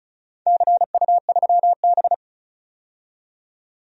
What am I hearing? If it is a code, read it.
Morse code, heard as a keyed tone, about 36 wpm, 700 Hz. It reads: CU3B
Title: DXCC-100 Call Signs - 4 Characters - 35wpm